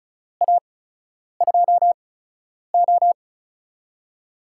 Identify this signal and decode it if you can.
Morse code, heard as a keyed tone, about 35 wpm, spaced as for 10 wpm, 700 Hz. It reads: A2O
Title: USA Call Signs - 8x Character Spacing - 35wpm